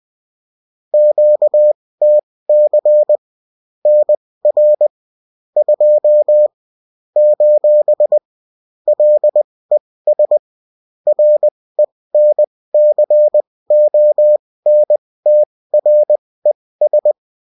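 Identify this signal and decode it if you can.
Morse code, heard as a keyed tone, about 20 wpm, spaced as for 16 wpm, 600 Hz. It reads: QTC NR 2 : LES RENCONTRES
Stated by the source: UFT bulletin sent in CW.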